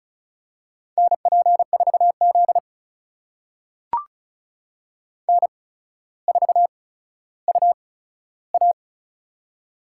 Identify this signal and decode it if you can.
Morse code, heard as a keyed tone, about 36 wpm, spaced as for 15 wpm, 700 Hz. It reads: NP4Z N4UA
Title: USA Call Signs - 4 Characters - 8x Character Spacing - 35wpm